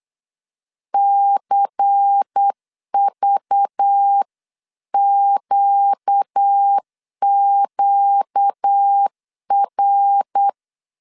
Morse code, heard as a keyed tone, about 8 wpm, 800 Hz. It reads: CVQQR
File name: 8wpm